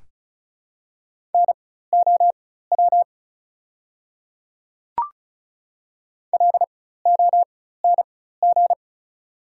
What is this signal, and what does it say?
Morse code, heard as a keyed tone, about 36 wpm, spaced as for 16 wpm, 690 Hz. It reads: NOW LONG
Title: 4x Character Spacing - Top 100 Words - 35wpm